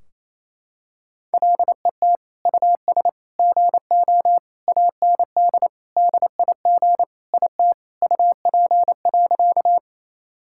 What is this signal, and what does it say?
Morse code, heard as a keyed tone, about 28 wpm, 700 Hz. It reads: LET US GO AND DIG IT UP.
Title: Sentences from Top 1600 Words 28wpm